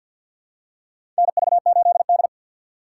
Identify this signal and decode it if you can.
Morse code, heard as a keyed tone, about 51 wpm, 700 Hz. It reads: NV8D
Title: USA Call Signs - 2x Character Spacing - 50wpm